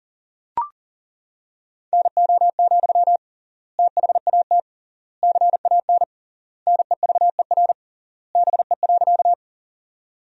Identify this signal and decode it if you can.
Morse code, heard as a keyed tone, about 37 wpm, 700 Hz. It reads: NO, THAT CAN NEVER BE.
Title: 3x Word-Spacing - Sentences from Top 300 Words - 40wpm